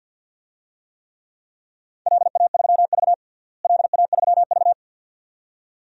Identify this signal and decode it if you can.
Morse code, heard as a keyed tone, about 51 wpm, 700 Hz. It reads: LA3V LA3V
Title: DXCC-100 Call Signs - 4 Characters - Repeated 2x - 50wpm